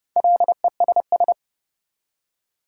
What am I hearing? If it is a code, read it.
Morse code, heard as a keyed tone, about 39 wpm, 700 Hz. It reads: LESS